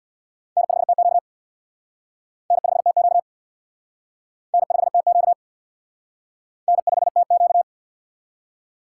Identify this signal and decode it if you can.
Morse code, heard as a keyed tone, about 66 wpm, 700 Hz. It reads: N5TQ N5TQ N5TQ N5TQ